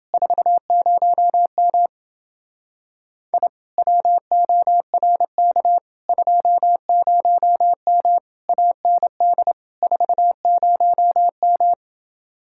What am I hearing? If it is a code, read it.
Morse code, heard as a keyed tone, about 30 wpm, 700 Hz. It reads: 40M I WORK 20M AND 40M